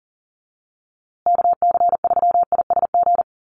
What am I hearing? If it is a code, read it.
Morse code, heard as a keyed tone, about 40 wpm, 700 Hz. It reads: KC3ISG
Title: USA Call Signs Repeated 3x 40wpm